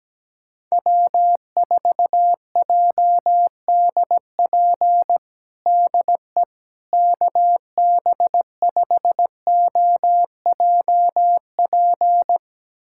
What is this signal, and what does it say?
Morse code, heard as a keyed tone, about 17 wpm, 700 Hz. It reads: W4JDP DE KB5OJP
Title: USA Callsign DE Callsign 17wpm